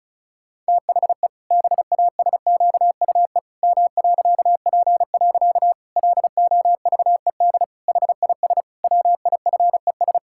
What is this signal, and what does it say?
Morse code, heard as a keyed tone, about 36 wpm, 700 Hz. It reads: THE BASQUE M.P. LOVED HIS WIFES
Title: Pangram Challenge v2 - 35wpm